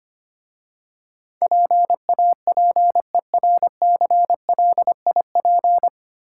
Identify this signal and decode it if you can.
Morse code, heard as a keyed tone, about 26 wpm, 700 Hz. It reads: PAPERCLIP
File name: Easy Words - Speed-Racing with Hints - 25wpm